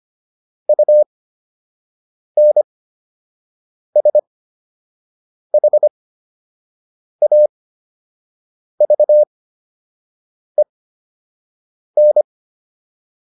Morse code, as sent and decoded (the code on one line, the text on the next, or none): ..- -. ... .... .- ...- . -.
UNSHAVEN